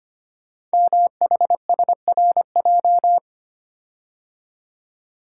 Morse code, heard as a keyed tone, about 25 wpm, 700 Hz. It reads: MHSRJ
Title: Code Groups - 5 Letters - 25wpm